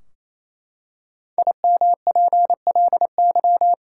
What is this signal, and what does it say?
Morse code, heard as a keyed tone, about 28 wpm, 700 Hz. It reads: IMPLY
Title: Five Letter Words - 28wpm